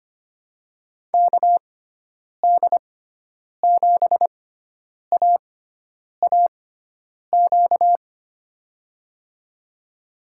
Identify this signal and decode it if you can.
Morse code, heard as a keyed tone, about 25 wpm, spaced as for 9 wpm, 700 Hz. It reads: KD7AAQ
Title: USA Call Signs - 6 Characters - 6x Character Spacing - 25wpm